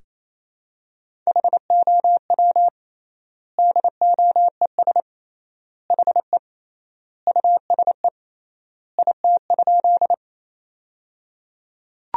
Morse code, heard as a keyed tone, about 28 wpm, 700 Hz. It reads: HOW DOES HE USE IT?
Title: 3x Word-Spacing - Sentences from Top 200 Words - 28wpm